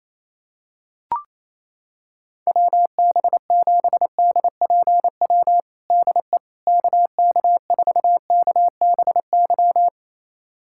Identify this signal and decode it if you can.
Morse code, heard as a keyed tone, about 28 wpm, 700 Hz. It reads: WB7DPW DE KK4KBY
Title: USA Callsign DE Callsign - 6 Characters - 28wpm